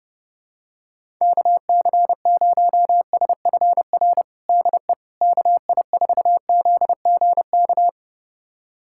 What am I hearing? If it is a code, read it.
Morse code, heard as a keyed tone, about 32 wpm, 700 Hz. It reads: KC0SFR DE KI4ZGK
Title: USA Callsign DE Callsign - 6 Characters - 30wpm